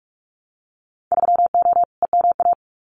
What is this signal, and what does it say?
Morse code, heard as a keyed tone, about 46 wpm, 700 Hz. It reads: 3O EMA
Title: Winter Field Day Exchanges 45wpm